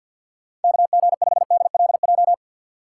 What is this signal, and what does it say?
Morse code, heard as a keyed tone, about 52 wpm, 700 Hz. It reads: KG5DLJ
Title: USA Call Signs - 6 Characters - 3x Character Spacing - 50wpm